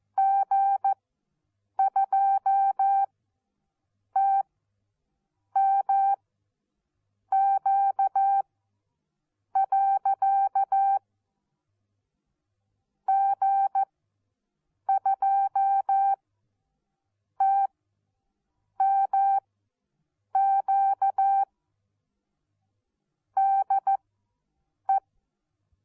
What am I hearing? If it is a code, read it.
Morse code, heard as a keyed tone, about 14 wpm, spaced as for 6 wpm, 780 Hz. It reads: G2TMQ. G2TMQ DE
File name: Onlinelehrgang / morsen / DJ4UF Morsen14